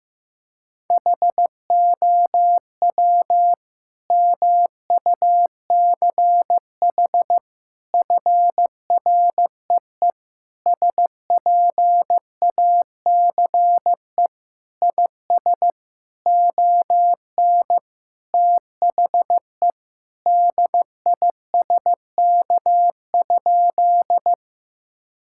Morse code, heard as a keyed tone, about 15 wpm, 700 Hz. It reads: HOW MUCH FREE SPACE IS ON THE DISK?
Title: Sentences from Top 1000 Words - Review Part 6 - 15wpm